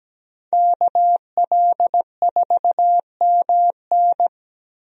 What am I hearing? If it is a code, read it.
Morse code, heard as a keyed tone, about 17 wpm, 700 Hz. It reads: KL4MN
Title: USA Call Signs - 5 Characters - Repeated 2x - 17wpm